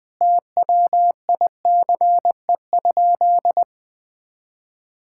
Morse code, heard as a keyed tone, about 20 wpm, 700 Hz. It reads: TWICE?
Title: Sentences from Top 1600 Words 20wpm